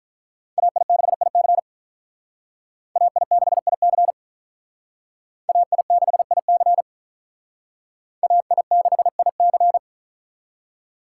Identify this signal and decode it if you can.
Morse code, heard as a keyed tone, about 44 wpm, 700 Hz. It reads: AI6IC AI6IC AI6IC AI6IC